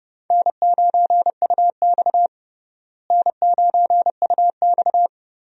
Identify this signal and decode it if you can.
Morse code, heard as a keyed tone, about 32 wpm, 700 Hz. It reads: N9UX N9UX
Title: USA Call Signs - 4 Characters - Repeated 3x - 30wpm